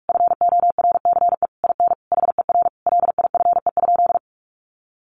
Morse code, heard as a keyed tone, about 44 wpm, 700 Hz. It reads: FORCE IN HER LIFE?
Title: Sentences from Top 1500 Words 45wpm